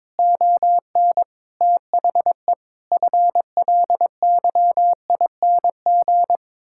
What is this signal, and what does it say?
Morse code, heard as a keyed tone, about 22 wpm, 700 Hz. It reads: ON THE FLYING
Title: Pangram Challenge v2 - 22wpm